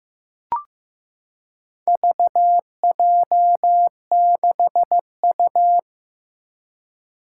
Morse code, heard as a keyed tone, about 15 wpm, 700 Hz. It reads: VJ6U
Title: Four Challenging Letter-Number Combinations 15wpm